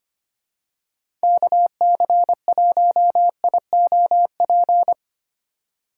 Morse code, heard as a keyed tone, about 25 wpm, 700 Hz. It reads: KC1IOP
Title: USA Call Signs - 6 Characters - 3x Character Spacing - 25wpm